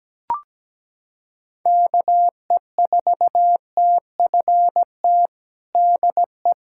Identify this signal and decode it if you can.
Morse code, heard as a keyed tone, about 17 wpm, 700 Hz. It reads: KE4TFT DE
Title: USA Callsign DE Callsign - 6 Characters - 17wpm